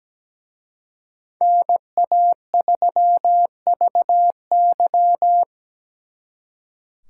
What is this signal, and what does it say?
Morse code, heard as a keyed tone, about 20 wpm, 700 Hz. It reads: NA3VY
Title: Speed Racing - USA 2x2 Call Signs 17wpm